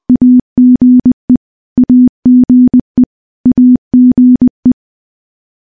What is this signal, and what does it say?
Morse code, heard as a keyed tone, about 21 wpm, 250 Hz. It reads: AGE AGE AGE
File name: morse / morse-training-20WPM